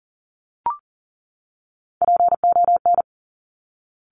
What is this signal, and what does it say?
Morse code, heard as a keyed tone, about 40 wpm, 700 Hz. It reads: PON
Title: Top Three Letter Combinations in Words 40wpm